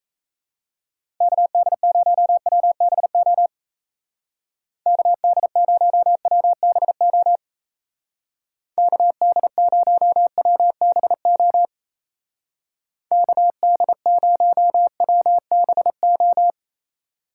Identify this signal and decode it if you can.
Morse code, heard as a keyed tone, about 34 wpm, 700 Hz. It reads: KD0WBO KD0WBO KD0WBO KD0WBO
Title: Speed Racing - USA Call Signs 28wpm